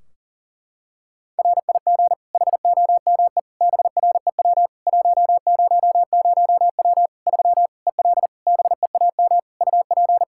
Morse code, heard as a keyed tone, about 40 wpm, 700 Hz. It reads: RIG HOME BREW 100W 3 EL BEAM UP
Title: Practice QSO - Exchange 2 - Info - 40wpm